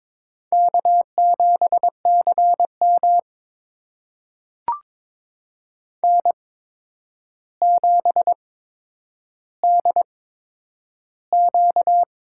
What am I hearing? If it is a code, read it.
Morse code, heard as a keyed tone, about 22 wpm, spaced as for 9 wpm, 700 Hz. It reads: K7CM N7DQ
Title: USA Call Signs - 4 Characters - 8x Character Spacing - 22wpm